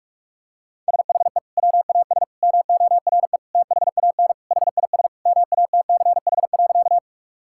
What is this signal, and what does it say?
Morse code, heard as a keyed tone, about 41 wpm, 700 Hz. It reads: SHE WAS MORE THAN HIS MATCH.
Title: Sentences from Top 1600 Words 45wpm